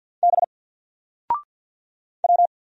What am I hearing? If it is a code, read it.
Morse code, heard as a keyed tone, about 50 wpm, 700 Hz. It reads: B W
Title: Single Letter-Number 50wpm